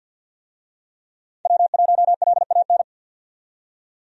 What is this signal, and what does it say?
Morse code, heard as a keyed tone, about 48 wpm, 700 Hz. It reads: W1LAN
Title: USA Call Signs - 5 Characters - 50wpm